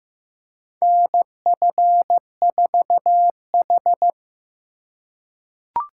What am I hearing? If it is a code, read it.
Morse code, heard as a keyed tone, about 15 wpm, 700 Hz. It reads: NF4H
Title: USA Call Signs - 4 Characters - 4x Character Spacing - 15wpm